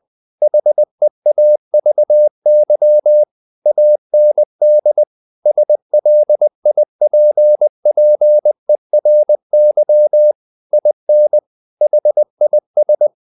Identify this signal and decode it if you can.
Morse code, heard as a keyed tone, about 20 wpm, 600 Hz. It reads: HEAVY AND SLIPPERY IN HIS
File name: the old man and the sea part 2 20wpm